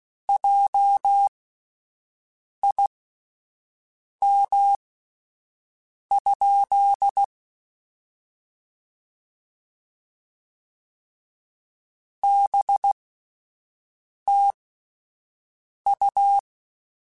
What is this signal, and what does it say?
Morse code, heard as a keyed tone, about 16 wpm, spaced as for 5 wpm, 800 Hz. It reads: JIM? BTU
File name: (spaced) / QSOs Spaced-mp3s / WD0BGP 1 @10 sp